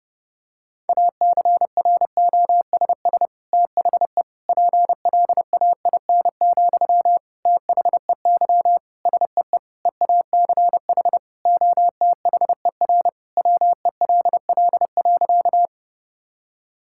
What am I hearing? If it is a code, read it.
Morse code, heard as a keyed tone, about 31 wpm, 700 Hz. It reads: ACROSS THE PLAIN, THEY SEE EACH OTHER WELL.
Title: Sentences from Top 1000 Words - Review Part 5 - 30wpm